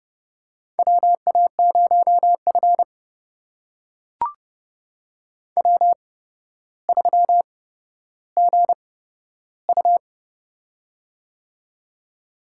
Morse code, heard as a keyed tone, about 31 wpm, spaced as for 13 wpm, 700 Hz. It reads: WA0F W3GU
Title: USA Call Signs - 4 Characters - 8x Character Spacing - 30wpm